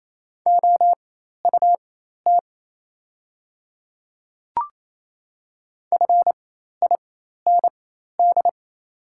Morse code, heard as a keyed tone, about 28 wpm, spaced as for 13 wpm, 700 Hz. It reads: OUT FIND
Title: 4x Character Spacing - Top 100 Words - 28wpm